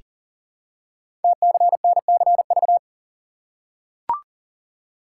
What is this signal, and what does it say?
Morse code, heard as a keyed tone, about 42 wpm, 700 Hz. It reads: TCNCV
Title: Code Groups - 5 Letters - 40wpm